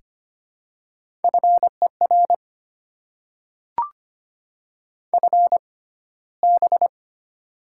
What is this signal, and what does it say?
Morse code, heard as a keyed tone, about 25 wpm, spaced as for 15 wpm, 700 Hz. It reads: FER FB
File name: Top 20 QSO Elements - 6x Character Spacing - 25wpm